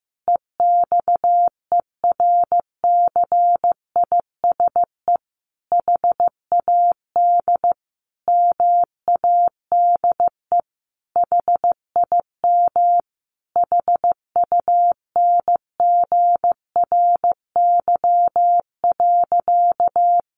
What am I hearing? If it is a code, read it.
Morse code, heard as a keyed tone, about 14 wpm, 700 Hz. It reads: EXERCISE HAD MADE HIM HUNGRY.
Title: Sentences from Top 1400 Words 15wpm